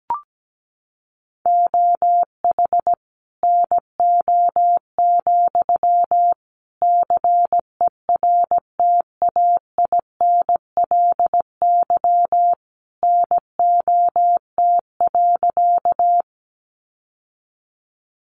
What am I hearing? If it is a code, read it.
Morse code, heard as a keyed tone, about 17 wpm, 700 Hz. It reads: OH NO, CERTAINLY NOT.
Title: Sentences from Top 900 Words 17wpm